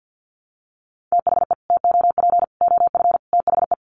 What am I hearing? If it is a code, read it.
Morse code, heard as a keyed tone, about 50 wpm, 700 Hz. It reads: THE TOP OF THE
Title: Sets of 5 Words Encore 50wpm